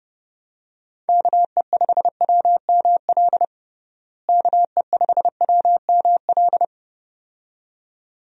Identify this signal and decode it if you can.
Morse code, heard as a keyed tone, about 32 wpm, 700 Hz. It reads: KE5WML KE5WML
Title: USA Call Signs - 6 Characters - Repeated 2x - 30wpm